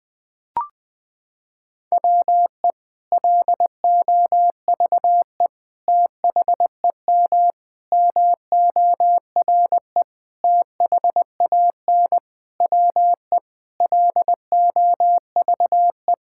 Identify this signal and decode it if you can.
Morse code, heard as a keyed tone, about 20 wpm, 700 Hz. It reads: WE LOVE THEM MORE THAN WE LOVE
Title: Sentences from Top 1600 Words 20wpm